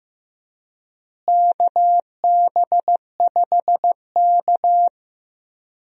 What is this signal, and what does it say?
Morse code, heard as a keyed tone, about 15 wpm, 700 Hz. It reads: KB5K
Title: USA Call Signs - 4 Characters - Repeated 3x - 15wpm